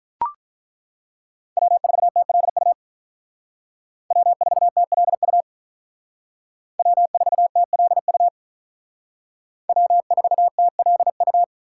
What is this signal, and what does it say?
Morse code, heard as a keyed tone, about 42 wpm, 700 Hz. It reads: W4TLU W4TLU W4TLU W4TLU
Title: Speed Racing - USA 1x3 Call Signs 35wpm